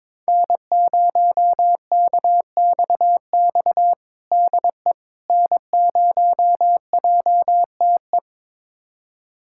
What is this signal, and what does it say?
Morse code, heard as a keyed tone, about 22 wpm, 700 Hz. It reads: N0KXX DE N0JTE
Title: USA Callsign DE Callsign - 5 Characters - 22wpm